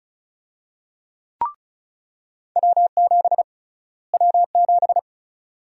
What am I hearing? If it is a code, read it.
Morse code, heard as a keyed tone, about 36 wpm, 700 Hz. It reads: W7 W7
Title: USA Call Sign Prefixes - Repeated 2x - 35wpm